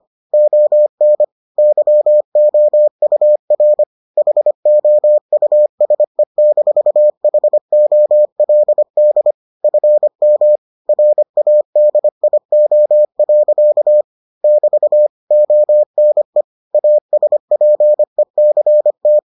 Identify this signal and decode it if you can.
Morse code, heard as a keyed tone, about 25 wpm, 600 Hz. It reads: ON YOUR HOUSE-HOLD FM RADIO. <BT> ONE ASPECT